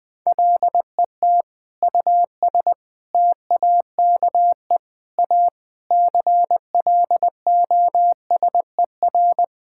Morse code, heard as a keyed tone, about 20 wpm, 700 Hz. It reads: LET US TAKE A CLOSER
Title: Sentences with 6-Word Phrases - 20wpm